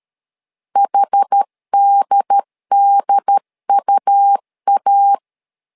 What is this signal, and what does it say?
Morse code, heard as a keyed tone, about 13 wpm, 800 Hz. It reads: HDDUA
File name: LEZIONI / 2 UHDN / 2 UHD 12wpm 1540